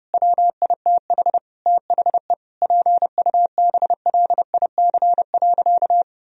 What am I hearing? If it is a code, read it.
Morse code, heard as a keyed tone, about 30 wpm, 700 Hz. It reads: WITH THE PUBLIC.
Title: Sentences from Top 1000 Words - Review Part 3 - 30wpm